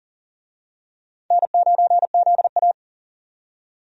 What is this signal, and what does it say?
Morse code, heard as a keyed tone, about 40 wpm, 700 Hz. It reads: N9ZA
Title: USA Call Signs - 4 Characters - 40wpm